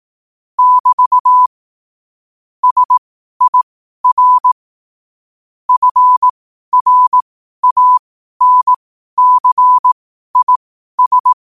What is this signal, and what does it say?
Morse code, heard as a keyed tone, about 18 wpm, spaced as for 12 wpm, 1000 Hz. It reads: <BT> SIR FRANCIS
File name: qotdfiles / 12wpm